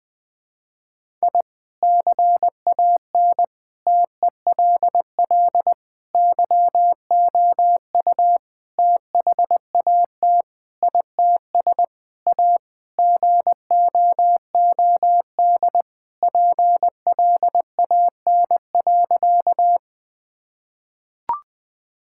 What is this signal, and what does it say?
Morse code, heard as a keyed tone, about 20 wpm, 700 Hz. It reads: I CAN TELL YOU THAT ITS A GOOD PLAN.
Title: Sentences with 5-Word Phrases - 20wpm